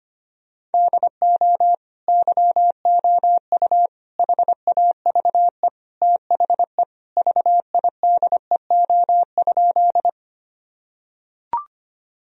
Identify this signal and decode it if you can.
Morse code, heard as a keyed tone, about 25 wpm, 700 Hz. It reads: DO YOU HAVE THE VIDEO?
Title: Sentences from Top 1000 Words - Review Part 4 - 25wpm